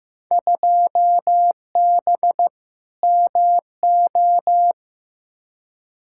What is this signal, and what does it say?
Morse code, heard as a keyed tone, about 15 wpm, 700 Hz. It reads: 2B MO
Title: Field Day Exchanges Encore - 15wpm